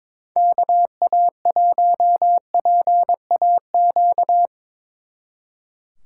Morse code, heard as a keyed tone, about 24 wpm, 700 Hz. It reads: KA1PAQ
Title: Speed Racing - USA 2x3 Call Signs 22wpm